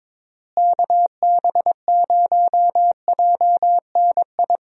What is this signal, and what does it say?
Morse code, heard as a keyed tone, about 22 wpm, 700 Hz. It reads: KB0JNI
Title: USA Call Signs - 6 Characters - Repeated 2x - 22wpm